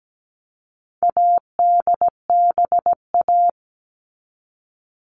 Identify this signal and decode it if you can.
Morse code, heard as a keyed tone, about 17 wpm, 700 Hz. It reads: ADBA
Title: All Cut Numbers - 4-Digit 17wpm